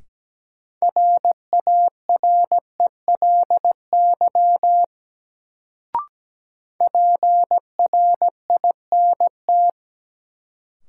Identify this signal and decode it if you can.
Morse code, heard as a keyed tone, about 17 wpm, 700 Hz. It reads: RARELY PRINT
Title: Sentences from Top 1400 Words - New Word Drill - 17wpm